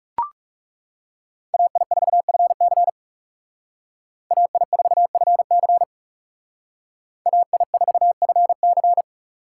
Morse code, heard as a keyed tone, about 38 wpm, 700 Hz. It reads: AI4FC AI4FC AI4FC